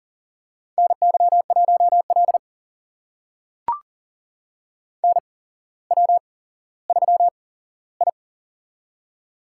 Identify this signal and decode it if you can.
Morse code, heard as a keyed tone, about 42 wpm, spaced as for 17 wpm, 690 Hz. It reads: NY1L NW3I
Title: USA Call Signs - 4 Characters - 8x Character Spacing - 40wpm